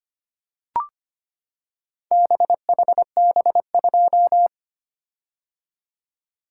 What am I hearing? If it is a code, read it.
Morse code, heard as a keyed tone, about 25 wpm, 700 Hz. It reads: BHB2
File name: practice / 25wpm / Four Letter-Number Combinations - 25wpm